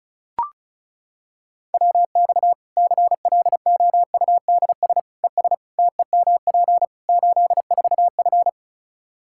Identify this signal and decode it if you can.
Morse code, heard as a keyed tone, about 34 wpm, 700 Hz. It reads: WX CLOUDS ES TEMP 84F
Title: Practice QSO - Exchange 2 - Info - 35wpm